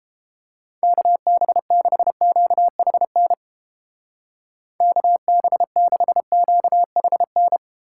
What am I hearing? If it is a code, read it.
Morse code, heard as a keyed tone, about 29 wpm, 700 Hz. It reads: KB6QHN KB6QHN